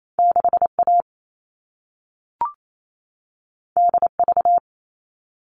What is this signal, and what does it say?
Morse code, heard as a keyed tone, about 28 wpm, 700 Hz. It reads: 6A DV
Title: All Cut Numbers - 2-Digit 28wpm